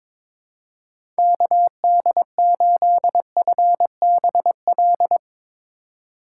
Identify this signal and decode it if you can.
Morse code, heard as a keyed tone, about 22 wpm, 700 Hz. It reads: KD8FBL
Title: USA Call Signs - 6 Characters - 2x Character Spacing - 22wpm